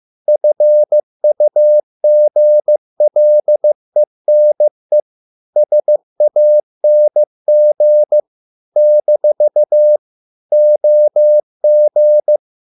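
Morse code, heard as a keyed tone, about 15 wpm, 600 Hz. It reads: FUGLENE SANG - OG
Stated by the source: Morse code at a slow 15 wpm from CW med Gnister.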